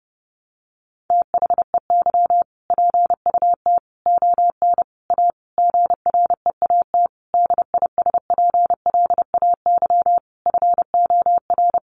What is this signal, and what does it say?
Morse code, heard as a keyed tone, about 30 wpm, 700 Hz. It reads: THEY PUT ON A GREAT DISPLAY FOR
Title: Sentences from Top 1200 Words 30wpm